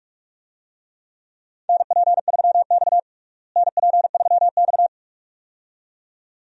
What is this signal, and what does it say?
Morse code, heard as a keyed tone, about 47 wpm, 700 Hz. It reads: NP3X NP3X
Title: USA Call Signs - 4 Characters - Repeated 2x - 45wpm